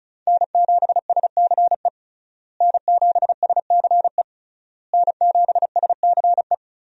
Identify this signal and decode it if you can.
Morse code, heard as a keyed tone, about 34 wpm, 700 Hz. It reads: N7SCE N7SCE N7SCE
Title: USA Call Signs - 5 Characters - Repeated 3x - 35wpm